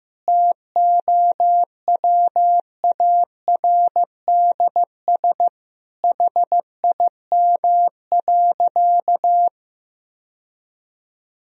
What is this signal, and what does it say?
Morse code, heard as a keyed tone, about 15 wpm, 700 Hz. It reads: TOWARDS HIM.
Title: Sentences from Top 1000 Words - Review Part 2 - 15wpm